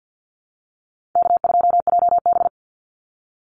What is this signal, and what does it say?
Morse code, heard as a keyed tone, about 48 wpm, 700 Hz. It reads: K2JB
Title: NA SOTA Chaser Callsigns 50wpm